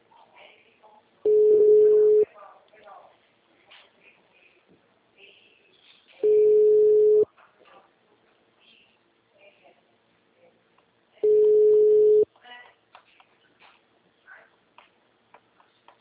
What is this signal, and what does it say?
Ringback tone (Europe, 425 Hz), with speech.